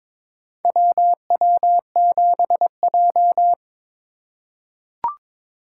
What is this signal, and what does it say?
Morse code, heard as a keyed tone, about 22 wpm, 700 Hz. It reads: WW7J
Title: USA Call Signs - 4 Characters - 3x Character Spacing - 22wpm